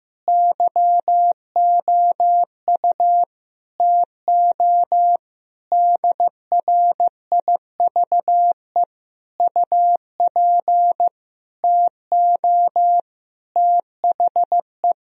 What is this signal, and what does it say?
Morse code, heard as a keyed tone, about 15 wpm, 700 Hz. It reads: YOU TO DRIVE UP TO THE
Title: Sentences from Top 1000 Words - Review Part 2 - 15wpm